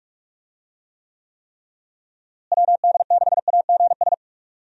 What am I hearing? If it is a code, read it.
Morse code, heard as a keyed tone, about 41 wpm, 700 Hz. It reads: WD6AGS
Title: USA Call Signs - 6 Characters - Repeated 3x - 45wpm